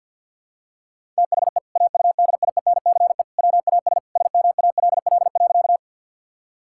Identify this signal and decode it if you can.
Morse code, heard as a keyed tone, about 50 wpm, 700 Hz. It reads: THE AUDIENCE WAS SMALL.
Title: Sentences from Top 1000 Words - Review Part 5 - 50wpm